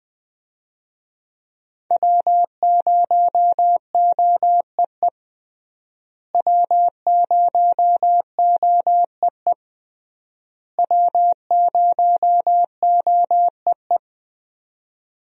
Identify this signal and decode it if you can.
Morse code, heard as a keyed tone, about 20 wpm, 700 Hz. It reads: W0OEE W0OEE W0OEE
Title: USA Call Signs - 5 Characters - Repeated 3x - 20wpm